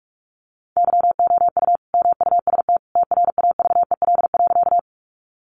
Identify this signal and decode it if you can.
Morse code, heard as a keyed tone, about 44 wpm, 700 Hz. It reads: YOU MUST TRAVEL.
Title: Sentences from Top 400 Words 45wpm